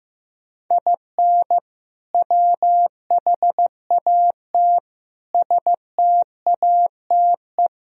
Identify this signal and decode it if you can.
Morse code, heard as a keyed tone, about 15 wpm, 700 Hz. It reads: IN WHAT STATE
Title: Sentences from Top 1000 Words - Review Part 2 - 15wpm